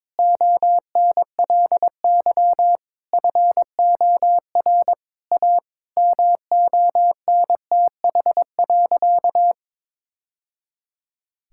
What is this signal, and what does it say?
Morse code, heard as a keyed tone, about 22 wpm, 700 Hz. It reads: ONLY FOR A MONTH.
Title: Sentences from Top 1000 Words - Review Part 5 - 22wpm